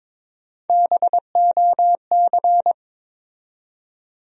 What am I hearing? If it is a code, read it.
Morse code, heard as a keyed tone, about 22 wpm, 700 Hz. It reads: BOC
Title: USA Call Sign Postfixes - 3 Letters 22wpm